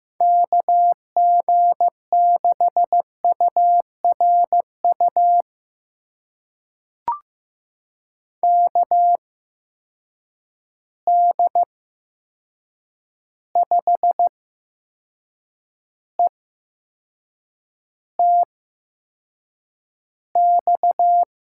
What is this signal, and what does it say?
Morse code, heard as a keyed tone, about 15 wpm, spaced as for 6 wpm, 700 Hz. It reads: KG6URU KD5ETX
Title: USA Call Signs - 6 Characters - 8x Character Spacing - 15wpm